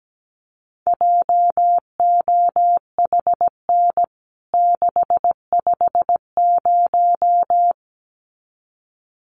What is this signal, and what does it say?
Morse code, heard as a keyed tone, about 17 wpm, 700 Hz. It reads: JOHN 650
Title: CWT Exchange 17wpm